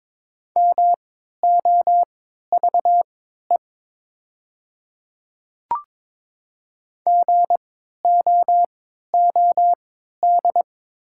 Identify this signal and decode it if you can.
Morse code, heard as a keyed tone, about 21 wpm, spaced as for 12 wpm, 700 Hz. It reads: MOVE GOOD
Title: 3x Character Spacing - Top 200 Words - 22wpm